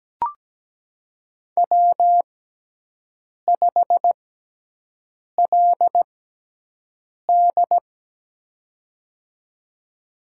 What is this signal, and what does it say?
Morse code, heard as a keyed tone, about 17 wpm, spaced as for 6 wpm, 700 Hz. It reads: W5LD
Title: USA Call Signs - 6x Character Spacing - 17wpm